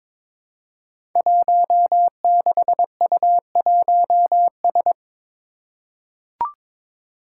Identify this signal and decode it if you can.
Morse code, heard as a keyed tone, about 23 wpm, 700 Hz. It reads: 16U1S
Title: Five Challenging Letter-Number Combinations 22wpm